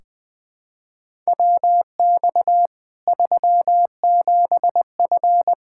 Morse code, heard as a keyed tone, about 21 wpm, 700 Hz. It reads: WX 37F
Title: Practice QSO - Exchange 2 - Info - 20wpm